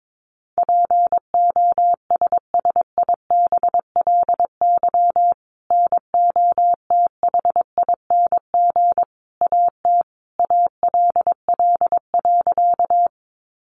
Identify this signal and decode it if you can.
Morse code, heard as a keyed tone, about 22 wpm, 700 Hz. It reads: POSSIBLY NOTHING AT ALL.
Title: Sentences from Top 1300 Words 22wpm